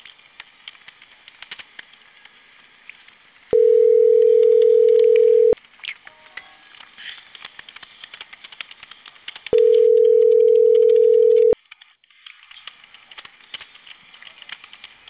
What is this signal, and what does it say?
Ringback tone (North America), 440 + 480 Hz.
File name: CENTRAL CAEA / out agcaea02 - .14984